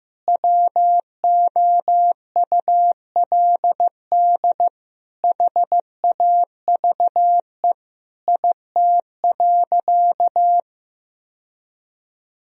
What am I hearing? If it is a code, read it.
Morse code, heard as a keyed tone, about 15 wpm, 700 Hz. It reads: WOULD HAVE IT.
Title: Sentences from Top 1000 Words - Review Part 3 - 15wpm